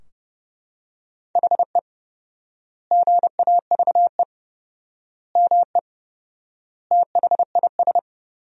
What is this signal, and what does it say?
Morse code, heard as a keyed tone, about 30 wpm, 700 Hz. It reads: HE GAVE ME THIS
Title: 4x Word-Spacing - Sentences from Top 500 Words - 30wpm